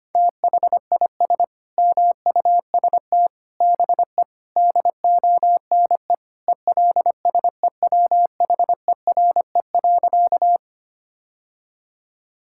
Morse code, heard as a keyed tone, about 25 wpm, 700 Hz. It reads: THIS MUST BE DONE ELSEWHERE.
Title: Sentences from Top 1600 Words 25wpm